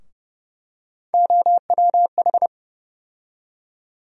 Morse code, heard as a keyed tone, about 31 wpm, 700 Hz. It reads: OWH
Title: USA Call Sign Postfixes - 3 Letters 30wpm